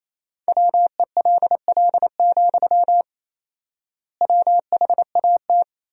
Morse code, heard as a keyed tone, about 28 wpm, 700 Hz. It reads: WELL, WHAT
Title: 4x Word-Spacing - Sentences from Top 200 Words - 28wpm